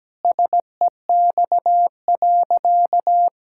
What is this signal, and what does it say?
Morse code, heard as a keyed tone, about 17 wpm, 700 Hz. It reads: SEX.
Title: Sentences from Top 1600 Words 17wpm